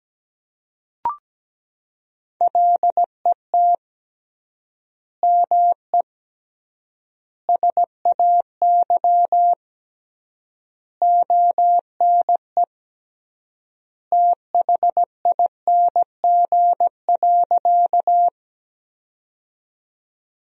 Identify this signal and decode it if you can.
Morse code, heard as a keyed tone, about 17 wpm, 700 Hz. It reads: LET ME SAY ONE THING.
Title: 3x Word-Spacing - Sentences from Top 300 Words - 17wpm